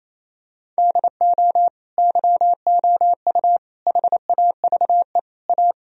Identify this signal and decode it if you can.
Morse code, heard as a keyed tone, about 28 wpm, 700 Hz. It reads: DO YOU HAVE A
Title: Sentences from Top 1000 Words - Review Part 4 - 28wpm